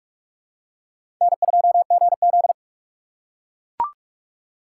Morse code, heard as a keyed tone, about 44 wpm, 690 Hz. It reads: N2GZ
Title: USA Call Signs - 4 Characters - 3x Character Spacing - 45wpm